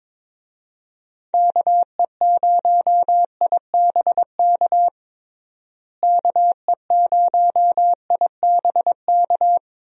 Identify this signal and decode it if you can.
Morse code, heard as a keyed tone, about 21 wpm, 700 Hz. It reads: KE0IBK KE0IBK
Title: USA Call Signs - 6 Characters - Repeated 2x - 22wpm